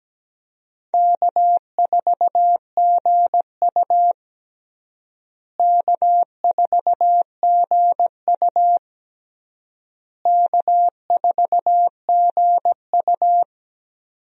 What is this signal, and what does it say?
Morse code, heard as a keyed tone, about 17 wpm, 700 Hz. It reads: K4GU K4GU K4GU
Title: USA Call Signs - 4 Characters - Repeated 3x - 17wpm